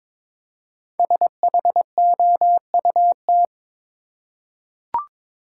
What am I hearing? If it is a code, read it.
Morse code, heard as a keyed tone, about 23 wpm, 700 Hz. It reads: SHOUT
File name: 5 Letter Words - First 3 Letters Given - 22wpm